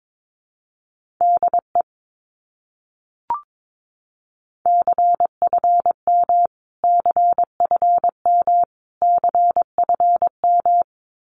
Morse code, heard as a keyed tone, about 22 wpm, 700 Hz. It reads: DE CFM CFM CFM
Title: Abbreviations Repeated 3x - 22wpm